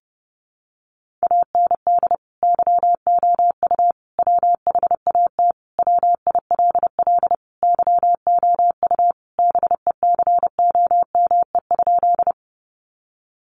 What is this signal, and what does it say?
Morse code, heard as a keyed tone, about 30 wpm, 700 Hz. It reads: AND YOU WHAT WILL YOU BECOME?
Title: Sentences from Top 600 Words 30wpm